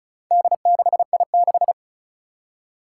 Morse code, heard as a keyed tone, about 34 wpm, 700 Hz. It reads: D6I6
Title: Four Challenging Letter-Number Combinations 35wpm